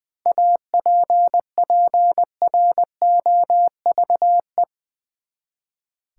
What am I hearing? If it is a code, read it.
Morse code, heard as a keyed tone, about 21 wpm, 700 Hz. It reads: APPROVE
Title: Seven Letter Words - 20wpm